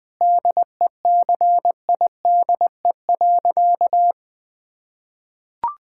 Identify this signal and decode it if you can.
Morse code, heard as a keyed tone, about 20 wpm, 700 Hz. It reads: DECIDE.
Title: Sentences with 8-Word Phrases - 20wpm